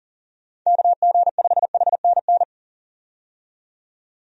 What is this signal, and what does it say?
Morse code, heard as a keyed tone, about 40 wpm, 700 Hz. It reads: KG5HNN
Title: USA Call Signs - 6 Characters - 40wpm